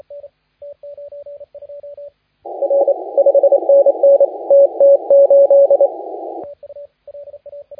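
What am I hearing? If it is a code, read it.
Morse code, heard as a keyed tone, about 27 wpm, 560 Hz, with radio noise. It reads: R 5NN TT8
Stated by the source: Morse contest audio recorded by LA4C during the SAC CW 2018 contest.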